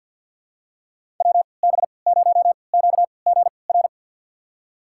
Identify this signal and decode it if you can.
Morse code, heard as a keyed tone, about 47 wpm, spaced as for 28 wpm, 700 Hz. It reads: WB0QGR
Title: USA Call Signs - 3x Character Spacing - 50wpm